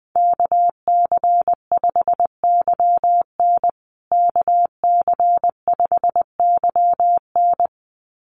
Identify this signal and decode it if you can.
Morse code, heard as a keyed tone, about 19 wpm, 700 Hz. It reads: KC5YN KC5YN
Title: USA Call Signs Repeated 2x 20wpm